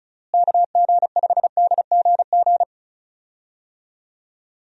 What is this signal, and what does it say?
Morse code, heard as a keyed tone, about 34 wpm, 700 Hz. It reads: KG5DGG
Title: USA Call Signs - 6 Characters - Repeated 3x - 35wpm